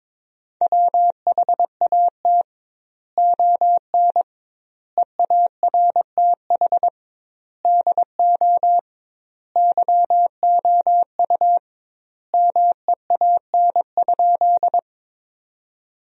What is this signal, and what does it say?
Morse code, heard as a keyed tone, about 22 wpm, 700 Hz. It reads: WHAT ON EARTH DO YOU MEAN?
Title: 2x Word-Spacing - Sentences from Top 200 Words - 22wpm